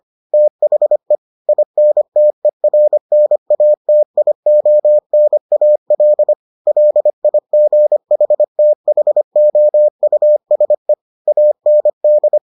Morse code, heard as a keyed tone, about 25 wpm, 600 Hz. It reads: THE INTERNATIONAL LIGHTHOUSE AND